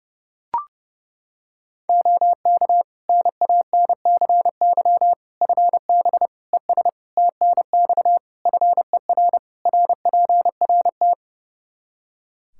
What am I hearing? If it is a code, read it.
Morse code, heard as a keyed tone, about 31 wpm, 700 Hz. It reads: OK NANCY FB ES TNX FER RPRT
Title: Practice QSO - Exchange 2 - Info - 30wpm